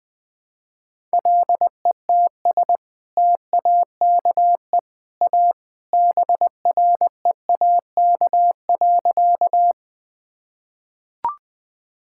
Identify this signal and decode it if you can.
Morse code, heard as a keyed tone, about 21 wpm, 700 Hz. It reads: LETS TAKE A BREAK.
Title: Sentences with 2-Word Phrases - Spoken Beginning - 20wpm